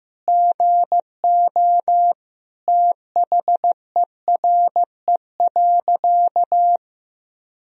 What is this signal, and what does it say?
Morse code, heard as a keyed tone, about 15 wpm, 700 Hz. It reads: GO THERE.
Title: Sentences from Top 1000 Words - Review Part 1 - 15wpm